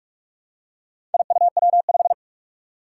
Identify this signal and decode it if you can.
Morse code, heard as a keyed tone, about 47 wpm, 700 Hz. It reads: IUW5
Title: Four Challenging Letter-Number Combinations 45wpm